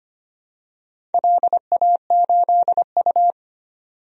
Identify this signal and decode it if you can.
Morse code, heard as a keyed tone, about 25 wpm, 700 Hz. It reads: LA8U
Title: DXCC-100 Call Signs - 4 Characters - 25wpm